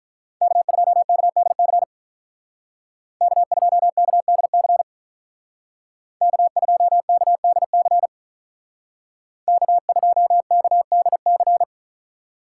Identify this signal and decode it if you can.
Morse code, heard as a keyed tone, about 45 wpm, 700 Hz. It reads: K2KDC K2KDC K2KDC K2KDC